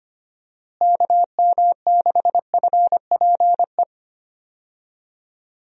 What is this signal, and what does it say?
Morse code, heard as a keyed tone, about 25 wpm, 700 Hz. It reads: KM6FPE
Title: USA Call Signs - 6 Characters - Repeated 2x - 25wpm